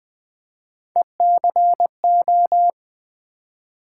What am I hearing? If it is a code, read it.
Morse code, heard as a keyed tone, about 20 wpm, 700 Hz. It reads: ECO
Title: Build A Word - 3 Letters At A Time - 9-Letter Words - 20wpm